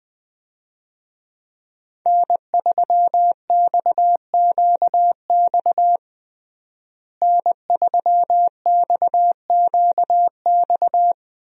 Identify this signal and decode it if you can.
Morse code, heard as a keyed tone, about 20 wpm, 700 Hz. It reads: N3XQX N3XQX
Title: USA Call Signs - 5 Characters - Repeated 2x - 20wpm